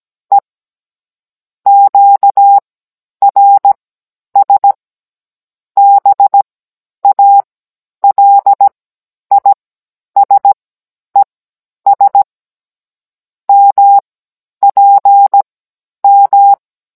Morse code, heard as a keyed tone, about 17 wpm, spaced as for 10 wpm, 800 Hz. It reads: E QRS BALISES MPM
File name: Cours-NR18-16mm